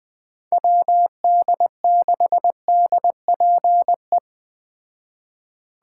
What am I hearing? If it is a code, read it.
Morse code, heard as a keyed tone, about 20 wpm, 700 Hz. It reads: WD6DPE
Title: Speed Racing - USA 2x3 Call Signs 17wpm